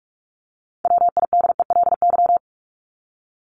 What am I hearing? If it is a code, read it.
Morse code, heard as a keyed tone, about 44 wpm, 700 Hz. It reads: WIDELY
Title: Top 1001-1250 Words 45wpm